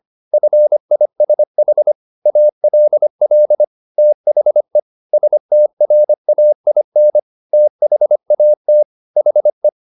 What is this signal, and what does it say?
Morse code, heard as a keyed tone, about 25 wpm, 600 Hz. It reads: FISH ALL THE STRAIN THAT HE